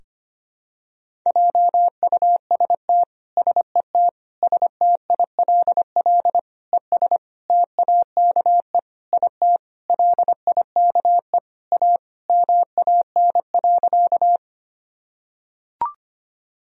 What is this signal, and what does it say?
Morse code, heard as a keyed tone, about 25 wpm, 700 Hz. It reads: JUST SET STILL ES TAKE IT LIKE A MAN.
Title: Sentences from Top 300 Words and Abbreviations 25wpm